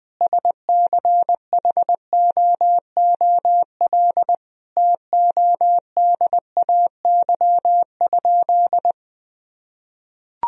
Morse code, heard as a keyed tone, about 21 wpm, 700 Hz. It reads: SCHOOL TODAY?
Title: Sentences from Top 1000 Words - Review Part 1 - 20wpm